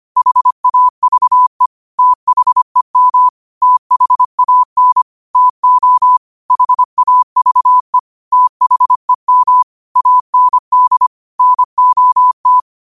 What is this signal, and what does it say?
Morse code, heard as a keyed tone, about 25 wpm, 1000 Hz. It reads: HAVE THEM THAN TO HAVE THEM AND NOT